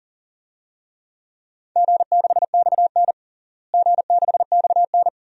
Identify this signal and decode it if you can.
Morse code, heard as a keyed tone, about 40 wpm, 700 Hz. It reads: G6XN G6XN
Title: DXCC-100 Call Signs - 4 Characters - Repeated 2x - 40wpm